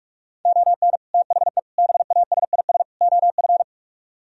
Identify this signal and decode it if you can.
Morse code, heard as a keyed tone, about 48 wpm, 690 Hz. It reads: ON THE BASIS OF